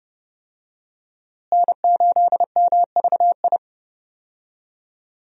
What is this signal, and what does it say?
Morse code, heard as a keyed tone, about 30 wpm, 700 Hz. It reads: N8MVI
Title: USA Call Signs - 5 Characters - Repeated 2x - 30wpm